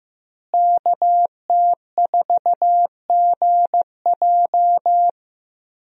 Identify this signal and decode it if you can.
Morse code, heard as a keyed tone, about 16 wpm, 700 Hz. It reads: KT4GJ
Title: USA Call Signs - 5 Characters - Repeated 2x - 15wpm